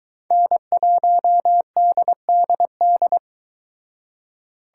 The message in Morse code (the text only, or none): N1DDD